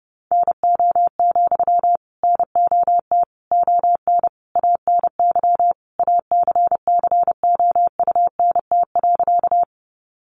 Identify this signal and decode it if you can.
Morse code, heard as a keyed tone, about 28 wpm, 700 Hz. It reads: NO, NOT ON ANY ACCOUNT.
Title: Sentences from Top 900 Words 30wpm